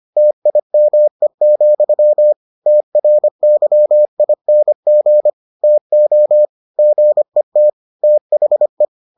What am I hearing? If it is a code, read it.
Morse code, heard as a keyed tone, about 25 wpm, 600 Hz. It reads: TIME, TRYING TO GET THE